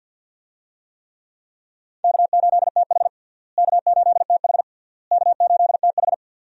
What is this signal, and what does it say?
Morse code, heard as a keyed tone, about 53 wpm, 700 Hz. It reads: K8TH K8TH K8TH
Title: USA Call Signs - 4 Characters - Repeated 3x - 50wpm